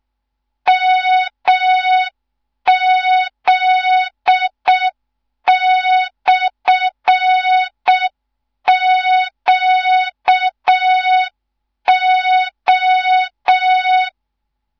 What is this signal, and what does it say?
Morse code, heard as a keyed tone, about 6 wpm, 3700 Hz. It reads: MZ/QO
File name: Morze-HA2MN / hang / 14 280 HA2MN